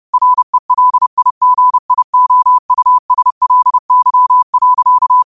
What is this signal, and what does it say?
Morse code, heard as a keyed tone, about 30 wpm, 1000 Hz. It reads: RELIGIOUSLY.
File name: qotdfiles / 30wpm